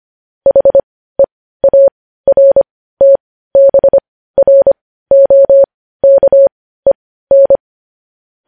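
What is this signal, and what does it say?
Morse code, heard as a keyed tone, about 25 wpm, spaced as for 15 wpm, 550 Hz. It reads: HEARTBROKEN
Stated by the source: Morse news headlines practice at 15 wpm.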